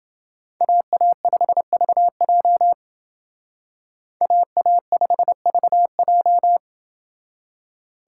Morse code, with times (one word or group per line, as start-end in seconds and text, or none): 0.6-2.7 AA5VJ
4.2-6.6 AA5VJ